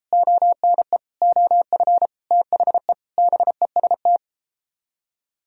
Morse code, heard as a keyed tone, about 32 wpm, 700 Hz. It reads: ONE OF THE BEST